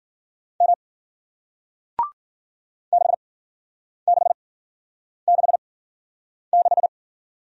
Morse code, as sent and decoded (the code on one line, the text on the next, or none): -- / -.... -.... -.... -....
M 6666